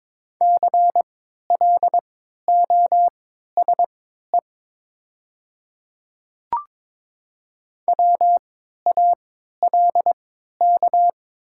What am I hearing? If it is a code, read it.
Morse code, heard as a keyed tone, about 22 wpm, spaced as for 12 wpm, 700 Hz. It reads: CLOSE WALK
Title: 3x Character Spacing - Top 300 Words - 22wpm